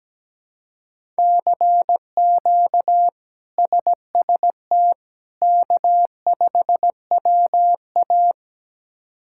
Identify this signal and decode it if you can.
Morse code, heard as a keyed tone, about 17 wpm, 700 Hz. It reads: CQ SST K5WA
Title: K1USN SST QSO Practice - 17wpm